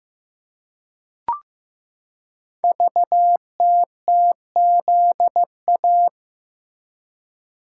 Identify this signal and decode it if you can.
Morse code, heard as a keyed tone, about 15 wpm, 700 Hz. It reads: VTTZA
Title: Code Groups - 5 Letters - 15wpm